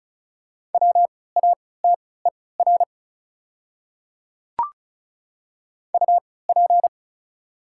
Morse code, heard as a keyed tone, about 36 wpm, spaced as for 20 wpm, 690 Hz. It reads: WATER UP
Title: 3x Character Spacing - Top 100 Words - 35wpm